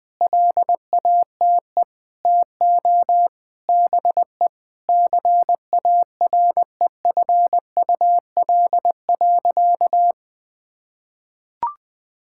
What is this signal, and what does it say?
Morse code, heard as a keyed tone, about 20 wpm, 700 Hz. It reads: LATE TO BE CAREFUL.
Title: Sentences from Top 1600 Words 20wpm